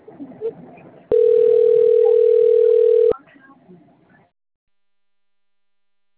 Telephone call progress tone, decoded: ringback tone

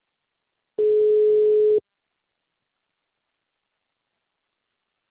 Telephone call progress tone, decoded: ringback tone

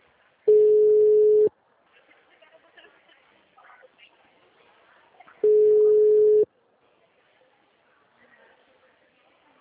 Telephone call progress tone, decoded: ringback tone

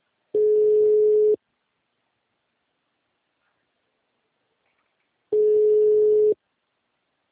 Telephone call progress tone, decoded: ringback tone

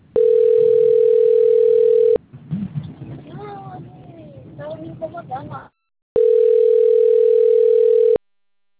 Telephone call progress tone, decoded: ringback tone

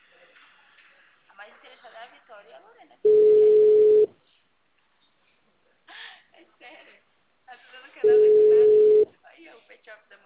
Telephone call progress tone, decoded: ringback tone

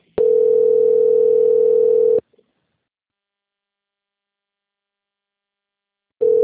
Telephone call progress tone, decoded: ringback tone

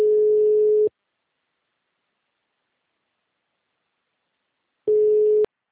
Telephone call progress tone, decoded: ringback tone

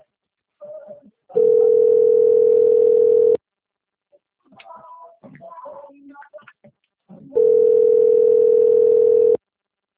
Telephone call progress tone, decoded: ringback tone

